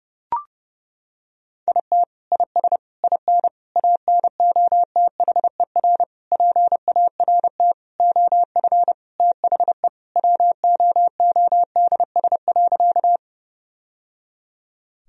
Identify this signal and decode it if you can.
Morse code, heard as a keyed tone, about 32 wpm, 700 Hz. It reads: IT IS IN ANOTHER PART OF THE WOODS.
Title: Sentences from Top 1000 Words - Review Part 2 - 30wpm